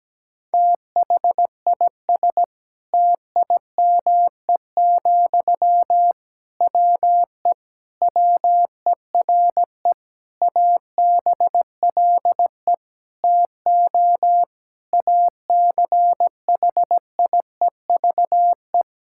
Morse code, heard as a keyed tone, about 17 wpm, 700 Hz. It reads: THIS TIME, WE WERE ABLE TO ACHIEVE
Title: Sentences from Top 1000 Words - Review Part 6 - 17wpm